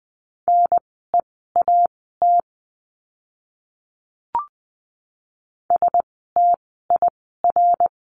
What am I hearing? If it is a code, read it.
Morse code, heard as a keyed tone, about 20 wpm, spaced as for 14 wpm, 700 Hz. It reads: NEAT STIR
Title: 054 Lesson 10f - Letter r - Words 20wpm